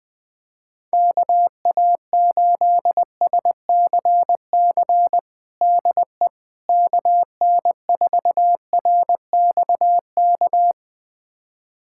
Morse code, heard as a keyed tone, about 20 wpm, 700 Hz. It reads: KA8SCC DE KN4RXK
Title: USA Callsign DE Callsign 20wpm